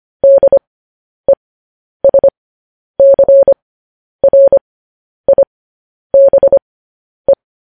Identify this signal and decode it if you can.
Morse code, heard as a keyed tone, about 25 wpm, spaced as for 10 wpm, 550 Hz. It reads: DESCRIBE
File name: podcast / News headlines-1x-25x10